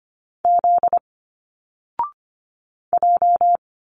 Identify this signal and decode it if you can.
Morse code, heard as a keyed tone, about 25 wpm, 700 Hz. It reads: Z J
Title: Single Letters 25wpm